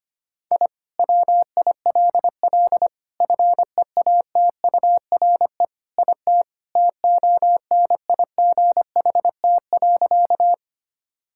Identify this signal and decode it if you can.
Morse code, heard as a keyed tone, about 25 wpm, 700 Hz. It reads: I WILL FEATURE IT TONIGHT.
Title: Sentences from Top 1000 Words - Review Part 5 - 25wpm